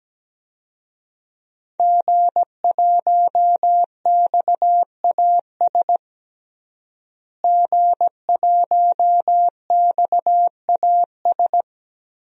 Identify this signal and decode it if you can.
Morse code, heard as a keyed tone, about 17 wpm, 700 Hz. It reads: G1XAS G1XAS
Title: DXCC-100 Call Signs - 5 Characters - Repeated 3x - 17wpm